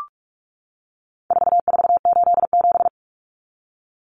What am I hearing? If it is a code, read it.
Morse code, heard as a keyed tone, about 44 wpm, 700 Hz. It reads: 4487
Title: Numbers - 4-Digit 45wpm